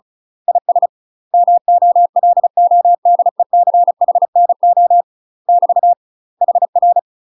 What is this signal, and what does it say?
Morse code, heard as a keyed tone, about 35 wpm, 700 Hz. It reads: IS MOLODECHNO <BT> HR
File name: UK MASTER QSOs TRAINING / 35WPM / UK MASTER QSOs 0005